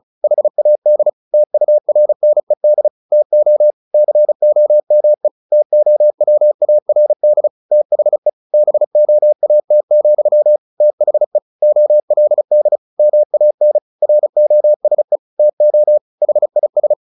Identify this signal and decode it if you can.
Morse code, heard as a keyed tone, about 34 wpm, 600 Hz. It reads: HAD TURNED TO COME TOWARD THE BOAT, THE OLD MAN ROSE TO HIS
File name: the old man and the sea 2 35wpm